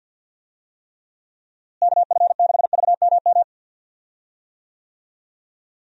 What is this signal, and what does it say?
Morse code, heard as a keyed tone, about 51 wpm, 700 Hz. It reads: KF6VMM
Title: USA Call Signs - 6 Characters - Repeated 3x - 50wpm